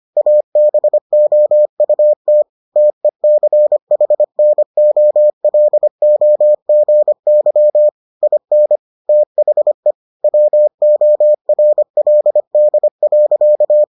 Morse code, heard as a keyed tone, about 25 wpm, 600 Hz. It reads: ABOUT TECHNOLOGY IN THE WORLD.